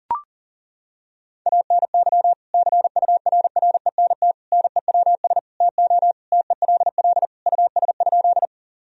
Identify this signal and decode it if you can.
Morse code, heard as a keyed tone, about 40 wpm, 700 Hz. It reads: ANY CURRENT NEWS TO TELL US?
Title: Sentences from Top 1000 Words - Review Part 4 - 40wpm